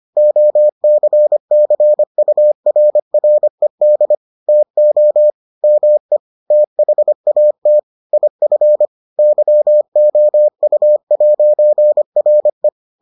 Morse code, heard as a keyed tone, about 25 wpm, 600 Hz. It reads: OCCURRED TO ME THAT IF YOU'RE